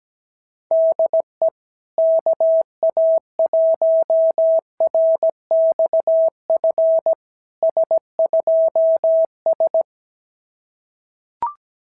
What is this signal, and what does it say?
Morse code, heard as a keyed tone, about 17 wpm, 660 Hz. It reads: DE KA1RXF S2S
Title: SOTA QSO Practice 17wpm